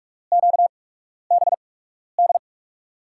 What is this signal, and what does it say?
Morse code, heard as a keyed tone, about 46 wpm, spaced as for 12 wpm, 700 Hz. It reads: QBD
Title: USA Call Signs - 6 Characters - 8x Character Spacing - 45wpm